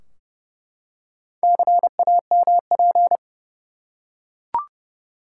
Morse code, heard as a keyed tone, about 30 wpm, 700 Hz. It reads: CAMP
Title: Four Letter Words - 30wpm